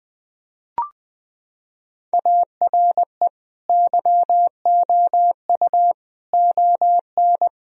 Morse code, heard as a keyed tone, about 20 wpm, 700 Hz. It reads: ARE YOU ON
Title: Sentences from Top 1000 Words - Review Part 2 - 20wpm